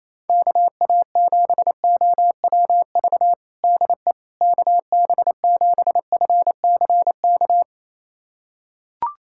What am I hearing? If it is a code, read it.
Morse code, heard as a keyed tone, about 28 wpm, 700 Hz. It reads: KA7OWV DE KB7FCK
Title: USA Callsign DE Callsign - 6 Characters - 28wpm